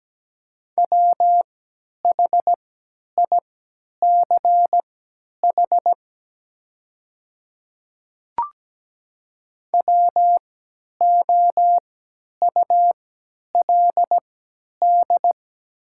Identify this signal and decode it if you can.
Morse code, heard as a keyed tone, about 17 wpm, spaced as for 10 wpm, 700 Hz. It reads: WHICH WOULD
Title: 3x Character Spacing - Top 100 Words - 17wpm